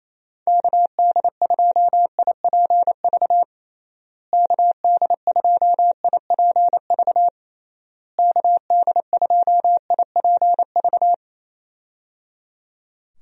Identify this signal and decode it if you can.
Morse code, heard as a keyed tone, about 28 wpm, 700 Hz. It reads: KD2IPV KD2IPV KD2IPV
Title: USA Call Signs - 6 Characters - Repeated 3x - 28wpm